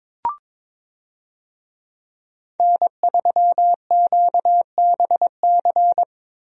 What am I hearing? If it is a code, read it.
Morse code, heard as a keyed tone, about 21 wpm, 700 Hz. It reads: N3QBC
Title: USA Call Signs - 5 Characters - Repeated 2x - 22wpm